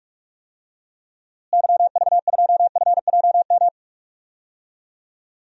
Morse code, heard as a keyed tone, about 43 wpm, 700 Hz. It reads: YV2FJM
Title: DXCC-100 Call Signs - 6 Characters - Repeated 2x - 45wpm